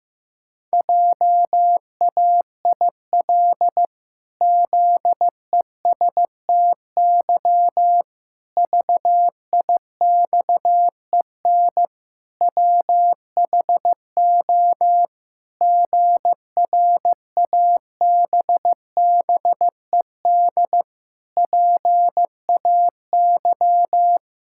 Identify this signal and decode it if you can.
Morse code, heard as a keyed tone, about 15 wpm, 700 Hz. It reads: JAIL ZESTY VIXEN WHO GRABBED PAY
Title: Pangram Challenge v2 - 15wpm